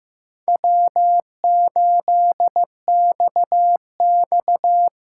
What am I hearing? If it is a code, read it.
Morse code, heard as a keyed tone, about 15 wpm, 700 Hz. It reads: W8XX
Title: USA Call Signs - 4 Characters - 4x Character Spacing - 15wpm